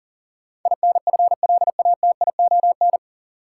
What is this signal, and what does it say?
Morse code, heard as a keyed tone, about 40 wpm, 700 Hz. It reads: INFLATION
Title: Sentences from Top 1100 Words - New Word Drill - 40wpm